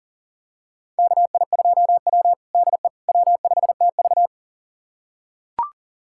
Morse code, heard as a keyed tone, about 40 wpm, 700 Hz. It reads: KI2W DE W5TV
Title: USA Callsign DE Callsign - 4 Characters - 40wpm